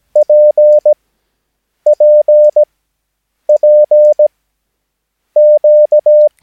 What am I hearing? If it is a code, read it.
Morse code, heard as a keyed tone, about 17 wpm, spaced as for 7 wpm, 600 Hz. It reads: PPPQ
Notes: Övningsmaterial med alfabetet A till Ö samt siffror 0 till 9 enligt International/Continental Morse Code.